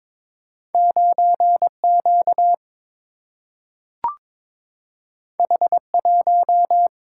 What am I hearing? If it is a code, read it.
Morse code, heard as a keyed tone, about 22 wpm, 700 Hz. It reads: 9Q H1
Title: Single-Letter Single-Number 22wpm